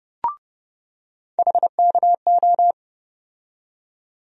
Morse code, heard as a keyed tone, about 32 wpm, 700 Hz. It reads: HKO
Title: USA Call Sign Postfixes - 3 Letters 30wpm